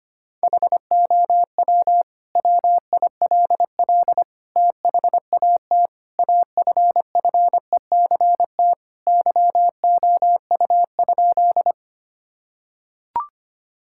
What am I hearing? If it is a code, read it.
Morse code, heard as a keyed tone, about 25 wpm, 700 Hz. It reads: HOW WILL THAT AFFECT YOU?
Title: Sentences from Top 1000 Words - Review Part 6 - 25wpm